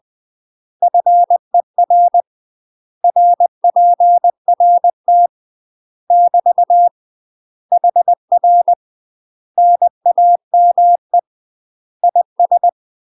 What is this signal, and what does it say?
Morse code, heard as a keyed tone, about 20 wpm, 700 Hz. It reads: FER RPRT <BT> HR NAME IS
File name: UK MASTER QSOs TRAINING / 20WPM / UK MASTER QSOs 0065